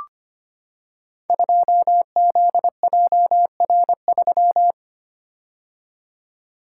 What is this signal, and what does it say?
Morse code, heard as a keyed tone, about 25 wpm, 700 Hz. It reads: 2ZJR3
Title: Code Groups - 5 Letter:Numbers - 25wpm